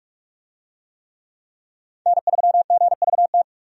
Morse code, heard as a keyed tone, about 44 wpm, 700 Hz. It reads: N3GVT
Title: USA Call Signs - 5 Characters - 45wpm